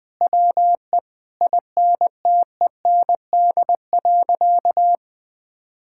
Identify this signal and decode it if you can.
Morse code, heard as a keyed tone, about 20 wpm, 700 Hz. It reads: WE INTEND.
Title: Sentences from Top 1600 Words 20wpm